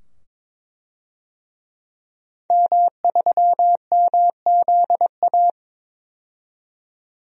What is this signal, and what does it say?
Morse code, heard as a keyed tone, about 22 wpm, 700 Hz. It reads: M3MZA
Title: DXCC-100 Call Signs - 5 Characters - 22wpm